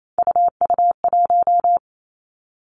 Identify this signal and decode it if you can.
Morse code, heard as a keyed tone, about 28 wpm, 700 Hz. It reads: UU1
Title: Three Challenging Letter-Number Combinations 28wpm